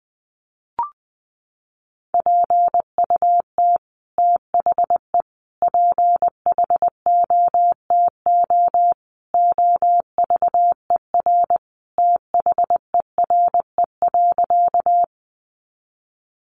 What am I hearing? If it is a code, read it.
Morse code, heard as a keyed tone, about 20 wpm, 700 Hz. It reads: PUT THE PHOTO OVER THERE.
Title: Sentences from Top 1200 Words 20wpm